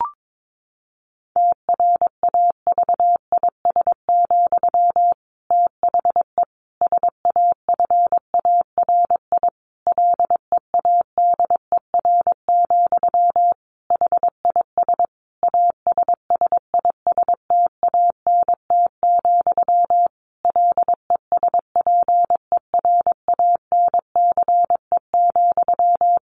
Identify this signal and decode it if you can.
Morse code, heard as a keyed tone, about 22 wpm, 700 Hz. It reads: TRAVIS, THE SAFARI LEADER, HIS ASSISTANT, LESPERANCE,
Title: A Sound Of Thunder 22wpm